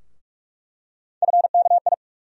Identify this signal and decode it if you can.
Morse code, heard as a keyed tone, about 46 wpm, 700 Hz. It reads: FKI
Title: USA Call Sign Postfixes - 3 Letters 45wpm